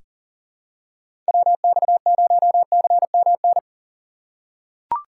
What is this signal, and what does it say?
Morse code, heard as a keyed tone, about 40 wpm, 700 Hz. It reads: WX0CMN
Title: USA Call Signs - 6 Characters - 4x Character Spacing - 40wpm